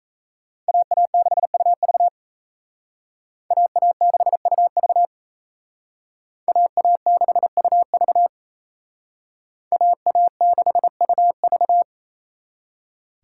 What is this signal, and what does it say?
Morse code, heard as a keyed tone, about 36 wpm, 700 Hz. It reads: AA6UV AA6UV AA6UV AA6UV